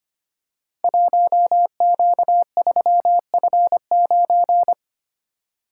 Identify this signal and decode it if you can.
Morse code, heard as a keyed tone, about 25 wpm, 700 Hz. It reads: 1Q3F9
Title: Code Groups - 5 Letter:Numbers - 25wpm